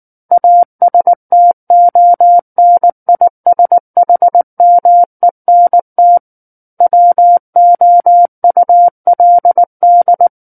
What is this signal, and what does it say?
Morse code, heard as a keyed tone, about 19 wpm, 700 Hz. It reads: ASTONISHMENT WOULD
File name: Book One / 10-Chapter 10 - 19 WPM